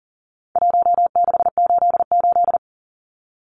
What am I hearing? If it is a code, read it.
Morse code, heard as a keyed tone, about 40 wpm, 700 Hz. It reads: 1688
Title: Numbers - Contest Style - 4-Digit 40wpm